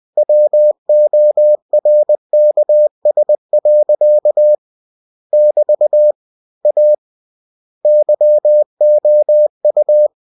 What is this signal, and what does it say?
Morse code, heard as a keyed tone, about 20 wpm, 600 Hz. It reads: WORKS. <BT> A YOU
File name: cw-books / winnie-the-pooh / winnie-20wpm-page-0117